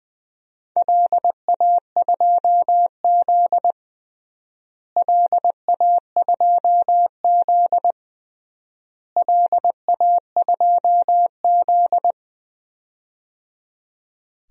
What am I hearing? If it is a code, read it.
Morse code, heard as a keyed tone, about 20 wpm, 700 Hz. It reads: LA2Z LA2Z LA2Z
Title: DXCC-100 Call Signs - 4 Characters - Repeated 3x - 20wpm